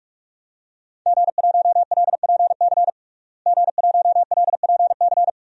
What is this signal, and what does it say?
Morse code, heard as a keyed tone, about 42 wpm, 700 Hz. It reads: G1LPC G1LPC
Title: DXCC-100 Call Signs - 5 Characters - Repeated 2x - 45wpm